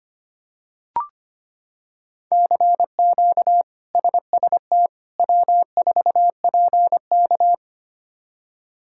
Morse code, heard as a keyed tone, about 25 wpm, 700 Hz. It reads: CQ SST W4PK
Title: K1USN SST QSO Practice - 25wpm